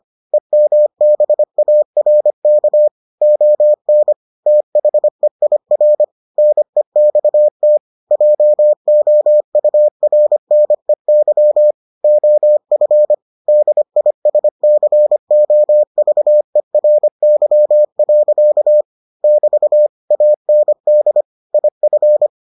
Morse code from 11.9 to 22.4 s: OF DISCOVERY. <BT> AND IF

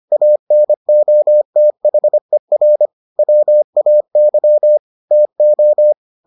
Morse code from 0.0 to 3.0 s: ANOTHER

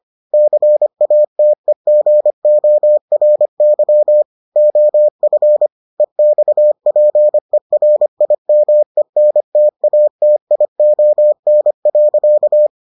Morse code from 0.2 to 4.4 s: CATEGORY

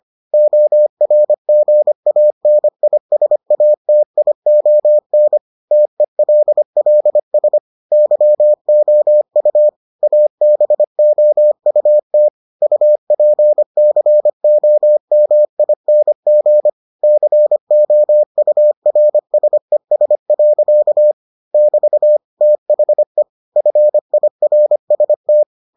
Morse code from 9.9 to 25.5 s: ABOUT UPCOMING COURSES. <BT> THE FIRST

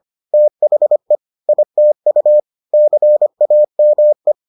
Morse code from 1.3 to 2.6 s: ITU

